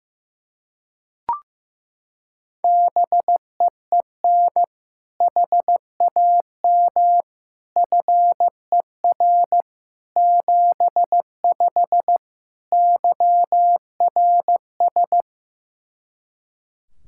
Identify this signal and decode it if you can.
Morse code, heard as a keyed tone, about 16 wpm, 700 Hz. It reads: BEEN HAM FER 75 YRS
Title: Practice QSO - Exchange 3 - More Info - 15wpm